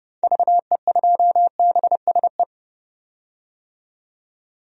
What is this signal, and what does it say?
Morse code, heard as a keyed tone, about 31 wpm, 700 Hz. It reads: VE2BSE
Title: DXCC-100 Call Signs - 6 Characters - 30wpm